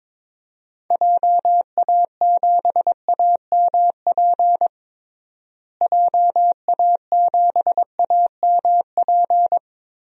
Morse code, heard as a keyed tone, about 22 wpm, 700 Hz. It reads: JA7AMP JA7AMP
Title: DXCC-100 Call Signs - 6 Characters - Repeated 3x - 22wpm